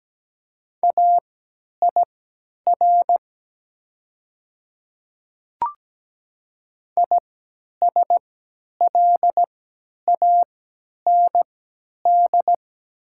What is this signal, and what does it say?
Morse code, heard as a keyed tone, about 17 wpm, spaced as for 10 wpm, 700 Hz. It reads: AIR ISLAND
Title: 3x Character Spacing - Top 500 Words - 17wpm